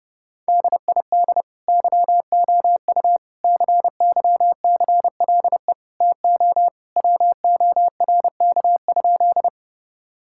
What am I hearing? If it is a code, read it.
Morse code, heard as a keyed tone, about 30 wpm, 700 Hz. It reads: DID YOU CYCLE TO WORK?
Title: Sentences from Top 1000 Words - Review Part 6 - 30wpm